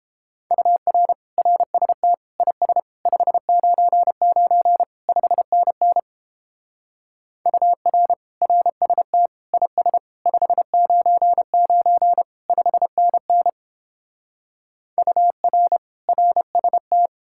Morse code, heard as a keyed tone, about 29 wpm, 700 Hz. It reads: UR RST IS 599 5NN UR RST IS 599 5NN UR RST